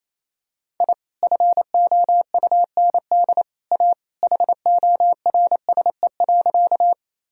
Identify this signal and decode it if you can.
Morse code, heard as a keyed tone, about 27 wpm, 700 Hz. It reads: I FOUND A HORSE.
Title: Sentences from Top 1000 Words - Review Part 1 - 28wpm